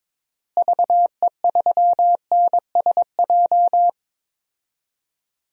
Morse code, heard as a keyed tone, about 22 wpm, 700 Hz. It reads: VE3NSJ
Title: DXCC-100 Call Signs - 6 Characters - Repeated 3x - 22wpm